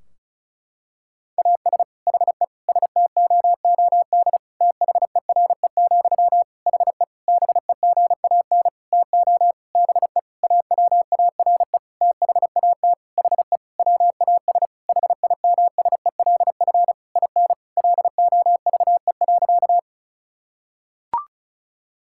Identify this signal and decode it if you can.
Morse code, heard as a keyed tone, about 36 wpm, 700 Hz. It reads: AS HE STOOD THERE, HE BEGAN TO BE AWARE THAT HE WAS HIMSELF IN LOVE.
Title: Sentences from Top 1600 Words 35wpm